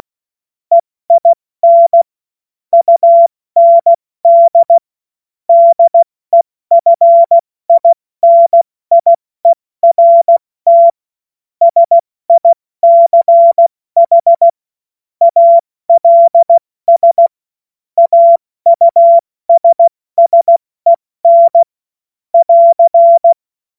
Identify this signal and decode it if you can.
Morse code, heard as a keyed tone, about 16 wpm, 680 Hz. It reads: EIN UND DEFINIERT SICH ALS AUSSEN <AR>
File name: Morsekurs / Witze / Witz 26 Dezember 2023 70 F80